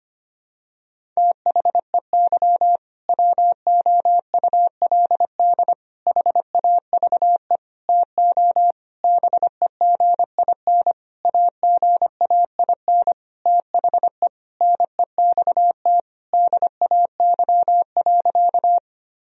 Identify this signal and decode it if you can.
Morse code, heard as a keyed tone, about 25 wpm, 700 Hz. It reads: THEY WOULD HAVE TO BEGIN AGAIN THE NEXT DAY.
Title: Sentences from Top 1000 Words - Review Part 1 - 25wpm